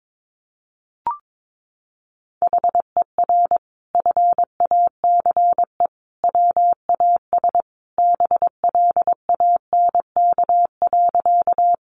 Morse code, heard as a keyed tone, about 22 wpm, 700 Hz. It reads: HER FACE WAS BLANK.
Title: Sentences from Top 1500 Words 22wpm